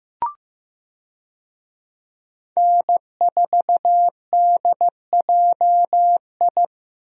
Morse code, heard as a keyed tone, about 15 wpm, 700 Hz. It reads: N4DJI
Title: USA Call Signs - 5 Characters - Repeated 3x - 15wpm